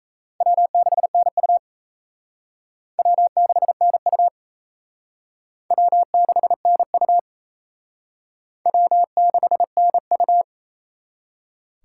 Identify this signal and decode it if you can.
Morse code, heard as a keyed tone, about 35 wpm, 700 Hz. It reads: W6NU W6NU W6NU W6NU
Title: Speed Racing - USA 1x2 Call Signs 28wpm